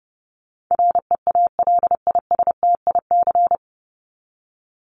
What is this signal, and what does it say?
Morse code, heard as a keyed tone, about 30 wpm, 700 Hz. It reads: REALISTIC
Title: Top 1751-2000 Words 30wpm